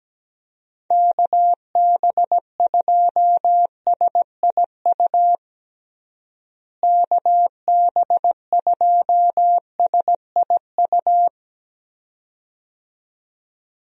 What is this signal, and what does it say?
Morse code, heard as a keyed tone, about 17 wpm, 700 Hz. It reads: KB2SIU KB2SIU
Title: USA Call Signs - 6 Characters - Repeated 2x - 17wpm